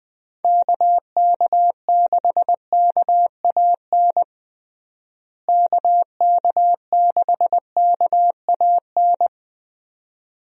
Morse code, heard as a keyed tone, about 20 wpm, 700 Hz. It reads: KK6KAN KK6KAN
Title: USA Call Signs - 6 Characters - Repeated 2x - 20wpm